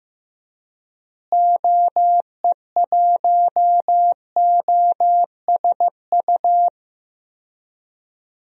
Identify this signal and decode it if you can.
Morse code, heard as a keyed tone, about 15 wpm, 700 Hz. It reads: OE1OSU
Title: DXCC-100 Call Signs - 6 Characters - Repeated 2x - 15wpm